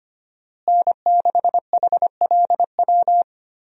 Morse code, heard as a keyed tone, about 25 wpm, 700 Hz. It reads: N6HLW
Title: USA Call Signs - 5 Characters - Repeated 3x - 25wpm